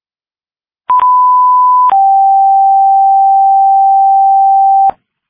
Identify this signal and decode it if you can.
Two-tone page, heard as a one-second tone, then a three-second tone, about 1000 then 780 Hz. It is heard from 0.9 to 4.9 s.